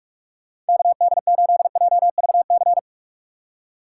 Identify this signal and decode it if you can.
Morse code, heard as a keyed tone, about 65 wpm, spaced as for 55 wpm, 700 Hz. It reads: KD8JVC
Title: Speed Racing - USA Call Signs 45wpm